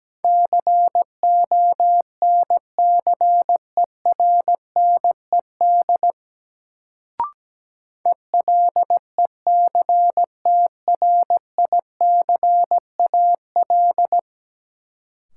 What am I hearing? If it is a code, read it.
Morse code, heard as a keyed tone, about 17 wpm, 700 Hz. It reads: CONCERNED ELECTRICAL
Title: Sentences from Top 1000 Words - Review Part 6 - New Word Drill - 17wpm